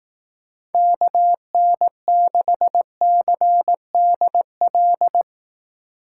Morse code, heard as a keyed tone, about 18 wpm, 700 Hz. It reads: KN6CDL
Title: Speed Racing - USA 2x3 Call Signs 15wpm